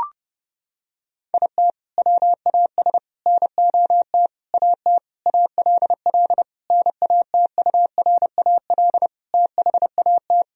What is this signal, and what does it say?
Morse code, heard as a keyed tone, about 30 wpm, 700 Hz. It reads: IT WAS NOT AT ALL NATURAL THAT
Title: Sentences from Top 1000 Words - Review Part 4 - 30wpm